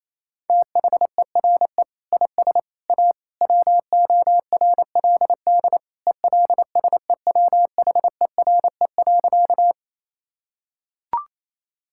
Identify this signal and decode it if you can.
Morse code, heard as a keyed tone, about 28 wpm, 700 Hz. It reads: THERE IS A WORLD ELSEWHERE.
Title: Sentences from Top 1600 Words 28wpm